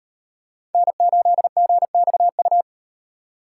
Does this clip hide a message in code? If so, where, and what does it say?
Yes: Morse code from 0.7 to 2.6 s, heard as a keyed tone, about 38 wpm, 700 Hz. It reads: N8GXU